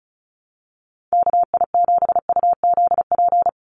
Morse code, heard as a keyed tone, about 34 wpm, 700 Hz. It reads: KI7UZP
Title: USA Call Signs 35wpm